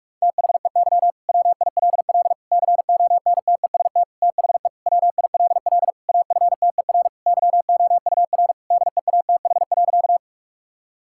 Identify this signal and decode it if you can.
Morse code, heard as a keyed tone, about 41 wpm, 700 Hz. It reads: THEY WILL CONTEST THE WILL AFTER YOUR DEATH.
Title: Sentences from Top 1600 Words 45wpm